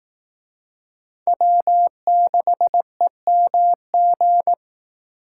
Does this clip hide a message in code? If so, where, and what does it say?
Yes: Morse code from 1.3 to 4.5 s, heard as a keyed tone, about 19 wpm, 700 Hz. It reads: W6EMG